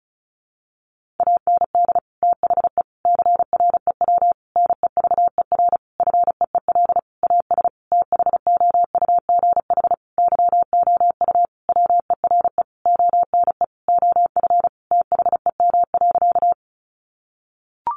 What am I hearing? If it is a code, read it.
Morse code, heard as a keyed tone, about 34 wpm, 700 Hz. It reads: AND THE CREW NEVER FEEL AS THOUGH YOU WERE ONE OF THEM.
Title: Sentences from Top 1500 Words 35wpm